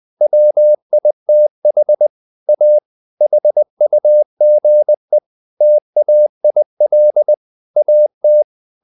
Morse code, heard as a keyed tone, about 20 wpm, 600 Hz. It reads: WITH A HUGE TAIL AT
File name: the old man and the sea part 2 20wpm